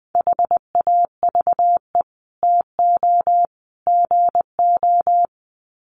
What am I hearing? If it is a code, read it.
Morse code, heard as a keyed tone, about 20 wpm, 700 Hz. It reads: HAVE TO GO
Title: Sets of 3 Words 20wpm